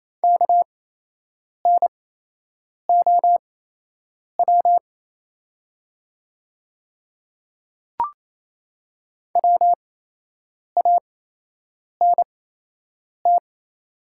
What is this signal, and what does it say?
Morse code, heard as a keyed tone, about 28 wpm, spaced as for 8 wpm, 700 Hz. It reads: KNOW WANT
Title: 8x Character Spacing - Top 300 Words - 28wpm